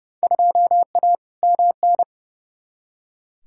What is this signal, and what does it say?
Morse code, heard as a keyed tone, about 30 wpm, 700 Hz. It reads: 2A MN
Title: Field Day Exchanges Encore - 30wpm